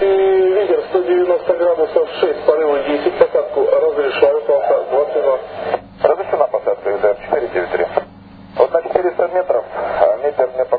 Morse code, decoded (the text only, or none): none